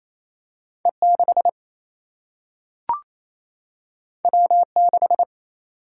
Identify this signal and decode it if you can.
Morse code, heard as a keyed tone, about 28 wpm, 700 Hz. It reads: E 6 W 6
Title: Single-Letter Single-Number 28wpm